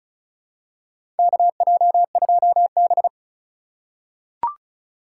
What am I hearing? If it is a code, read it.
Morse code, heard as a keyed tone, about 36 wpm, 690 Hz. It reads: KJ2B
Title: USA Call Signs - 3x Character Spacing - 35wpm